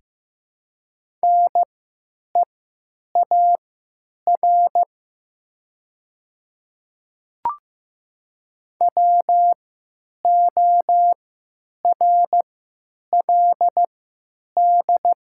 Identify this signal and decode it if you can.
Morse code, heard as a keyed tone, about 15 wpm, spaced as for 9 wpm, 700 Hz. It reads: NEAR WORLD
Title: 3x Character Spacing - Top 500 Words - 15wpm